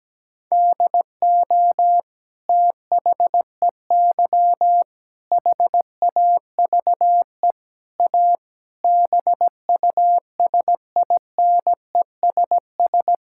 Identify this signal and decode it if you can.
Morse code, heard as a keyed tone, about 17 wpm, 700 Hz. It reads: DO THEY HAVE A BUSINESS
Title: Sentences from Top 1000 Words - Review Part 5 - 17wpm